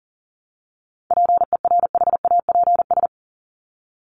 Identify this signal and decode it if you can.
Morse code, heard as a keyed tone, about 40 wpm, 700 Hz. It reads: PERHAPS
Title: Top 500 Words Repeated 40wpm